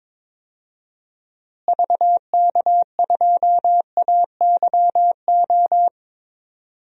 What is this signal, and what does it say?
Morse code, heard as a keyed tone, about 21 wpm, 700 Hz. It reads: VK2AYO
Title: DXCC-100 Call Signs - 6 Characters - Repeated 3x - 22wpm